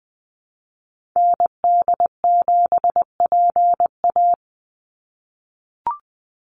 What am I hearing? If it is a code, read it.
Morse code, heard as a keyed tone, about 20 wpm, 700 Hz. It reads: ND7PA
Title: NA SOTA Activator Callsigns 20wpm